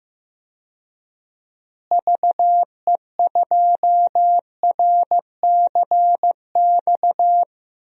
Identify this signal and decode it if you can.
Morse code, heard as a keyed tone, about 16 wpm, 700 Hz. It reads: VE2RCX
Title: DXCC-100 Call Signs - 6 Characters - Repeated 3x - 15wpm